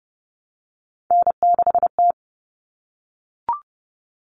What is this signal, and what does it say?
Morse code, heard as a keyed tone, about 30 wpm, 700 Hz. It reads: N6T
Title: All Cut Numbers - 3-Digit 30wpm